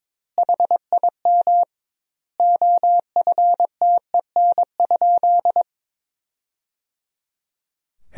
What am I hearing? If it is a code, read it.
Morse code, heard as a keyed tone, about 22 wpm, 700 Hz. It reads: HIM OFTEN?
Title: 2x Word-Spacing - Sentences from Top 400 Words - 22wpm